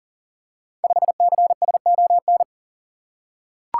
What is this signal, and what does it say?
Morse code, heard as a keyed tone, about 39 wpm, 700 Hz. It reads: 5CSON
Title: Code Groups - 5 Letter:Numbers - 40wpm